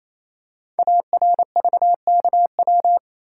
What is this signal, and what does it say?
Morse code, heard as a keyed tone, about 28 wpm, 700 Hz. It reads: ARVKW
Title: Code Groups - 5 Letters - 28wpm